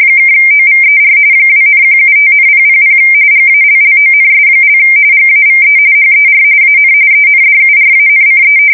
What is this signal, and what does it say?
RTTY, about 45.45 baud. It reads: (OVER 2,175,000 GALLONS) GAVE HER AN ENDURANCE
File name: fsk170 / battleshipnj